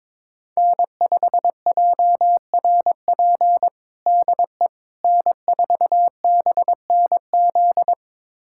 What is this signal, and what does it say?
Morse code, heard as a keyed tone, about 22 wpm, 700 Hz. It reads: N5JRP DE N4BNZ
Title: USA Callsign DE Callsign - 5 Characters - 22wpm